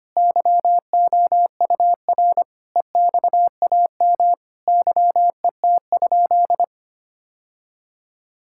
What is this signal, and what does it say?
Morse code, heard as a keyed tone, about 25 wpm, 700 Hz. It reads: YOUR EXAM YET?
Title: Sentences from Top 1000 Words - Review Part 5 - 25wpm